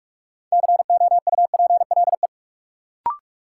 Morse code, heard as a keyed tone, about 45 wpm, 700 Hz. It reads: COUPLE
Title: Sets of 2 Words - New Word Drill - 45wpm